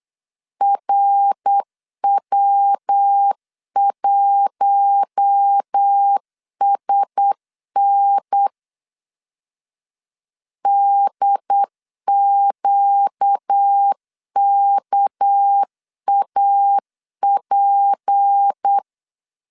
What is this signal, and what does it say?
Morse code, heard as a keyed tone, about 8 wpm, 800 Hz. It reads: RW1SN DQKAP
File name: LEZIONI / 6 PKX0 / 6 PKX0 8wpm 3341